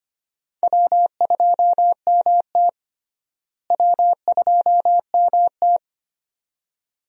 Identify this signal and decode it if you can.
Morse code, heard as a keyed tone, about 25 wpm, 700 Hz. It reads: W2MT W2MT
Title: USA Call Signs - 4 Characters - Repeated 2x - 25wpm